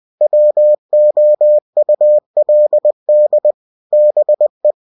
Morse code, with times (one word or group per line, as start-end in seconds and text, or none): 0.2-3.5 WOULD
3.9-4.7 BE